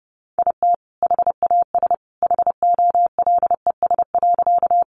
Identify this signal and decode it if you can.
Morse code, heard as a keyed tone, about 30 wpm, 700 Hz. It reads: IT HAS HOLES.
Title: Additional Easy Phrases 30wpm